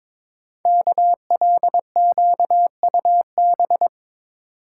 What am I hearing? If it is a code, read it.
Morse code, heard as a keyed tone, about 22 wpm, 700 Hz. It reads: KLQUB
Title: Code Groups - 5 Letter:Numbers - 22wpm